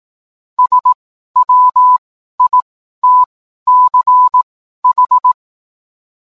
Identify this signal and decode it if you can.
Morse code, heard as a keyed tone, about 18 wpm, spaced as for 12 wpm, 1000 Hz. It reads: SWITCH